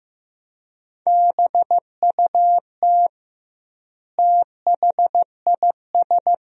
Morse code, heard as a keyed tone, about 15 wpm, 700 Hz. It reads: BUT THIS
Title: 2x Word-Spacing - Sentences from Top 200 Words - 15wpm